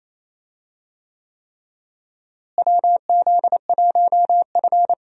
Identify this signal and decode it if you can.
Morse code, heard as a keyed tone, about 28 wpm, 700 Hz. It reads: WZ1F
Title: USA Call Signs - 4 Characters - Repeated 3x - 28wpm